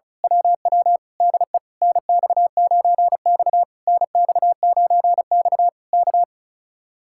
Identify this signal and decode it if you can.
Morse code, heard as a keyed tone, about 32 wpm, 700 Hz. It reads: WW DE NX9X NX9X K
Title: Speed-Racing Common QSO Phrases - 35wpm